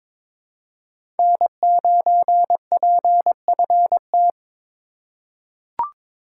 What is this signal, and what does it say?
Morse code, heard as a keyed tone, about 22 wpm, 700 Hz. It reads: N9PFT
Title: USA Call Signs - 5 Characters - 4x Character Spacing - 22wpm